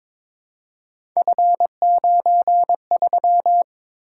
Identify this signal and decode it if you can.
Morse code, heard as a keyed tone, about 22 wpm, 700 Hz. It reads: F93
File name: practice / 22wpm / Three Letter-Number Combinations - 22wpm